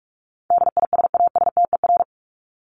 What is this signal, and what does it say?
Morse code, heard as a keyed tone, about 44 wpm, 700 Hz. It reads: DISASTER
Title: Top 1501-1750 Words 45wpm